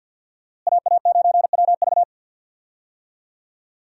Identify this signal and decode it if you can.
Morse code, heard as a keyed tone, about 51 wpm, 700 Hz. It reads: AA9WV
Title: USA Call Signs - 5 Characters - Repeated 3x - 50wpm